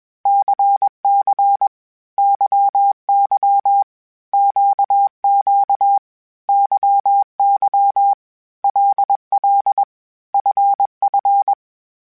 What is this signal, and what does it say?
Morse code, heard as a keyed tone, about 21 wpm, 800 Hz. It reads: CC YY QQ YY LL FF
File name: Basic Sets / Basic Letter Groups Collective7 / LFCYQ@20